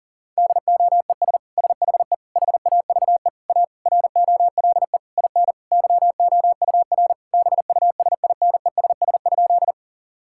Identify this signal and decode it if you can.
Morse code, heard as a keyed tone, about 40 wpm, 700 Hz. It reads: DOES SHE HAVE A ROLE IN YOUR BUSINESS?
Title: Sentences from Top 1000 Words - Review Part 6 - 40wpm